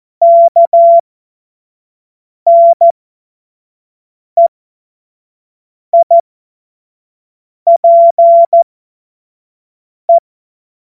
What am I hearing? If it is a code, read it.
Morse code, heard as a keyed tone, about 14 wpm, spaced as for 5 wpm, 680 Hz. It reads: KNEIPE
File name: Morsekurs / Witze / Witz 18. Mai 2021, 25 F70